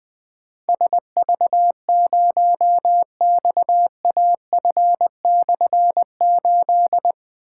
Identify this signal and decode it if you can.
Morse code, heard as a keyed tone, about 20 wpm, 700 Hz. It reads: SV0XAF/8
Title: Difficult Call Signs 20wpm